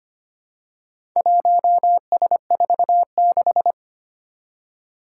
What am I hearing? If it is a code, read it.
Morse code, heard as a keyed tone, about 25 wpm, 700 Hz. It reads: 1S46
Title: Four Challenging Letter-Number Combinations 25wpm